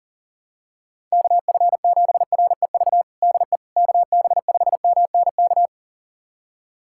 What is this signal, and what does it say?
Morse code, heard as a keyed tone, about 40 wpm, 700 Hz. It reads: KF7REV DE KB5MNK
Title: USA Callsign DE Callsign - 6 Characters - 40wpm